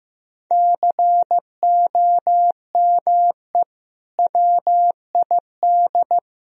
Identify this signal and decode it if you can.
Morse code, heard as a keyed tone, about 15 wpm, 700 Hz. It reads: COME WID
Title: Sentences from Top 200 Words and Abbreviations 15wpm